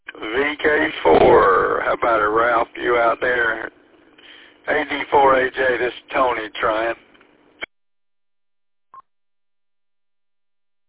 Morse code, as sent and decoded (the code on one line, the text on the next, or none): none